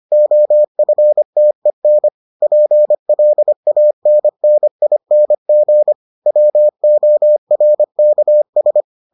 Morse code, heard as a keyed tone, about 25 wpm, 600 Hz. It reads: OFTEN PLANNING WORKS